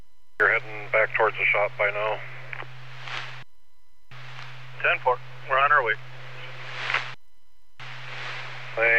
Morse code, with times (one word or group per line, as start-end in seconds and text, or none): none